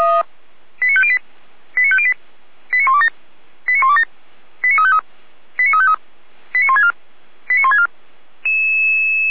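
Five-tone selective call (ZVEI), heard as quick runs of five tones, tones of about 70 ms. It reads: 89508 89508 89127 89127 89353 89353 89164 89164
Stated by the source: ZVEI five-tone selective call received on an SDR, archived by feuerwehr-orlamuende (Orlamünde fire brigade).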